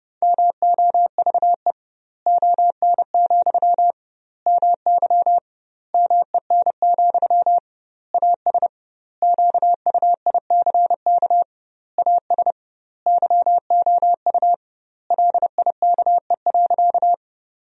Morse code, heard as a keyed tone, about 28 wpm, 700 Hz. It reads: MOVE ON, MY MEN, AS QUICK AS YOU LIKE.
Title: 2x Word-Spacing - Sentences from Top 500 Words - 30wpm